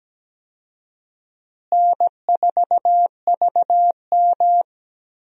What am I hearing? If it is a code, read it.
Morse code, heard as a keyed tone, about 17 wpm, 700 Hz. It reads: N4VM
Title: USA Call Signs - 4 Characters - Repeated 3x - 17wpm